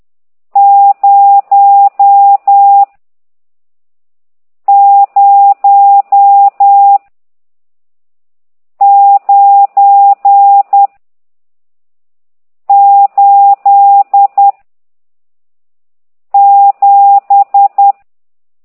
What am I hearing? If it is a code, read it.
Morse code, heard as a keyed tone, about 10 wpm, 800 Hz. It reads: 0 0 9 8 7
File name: ct2idw / exerc14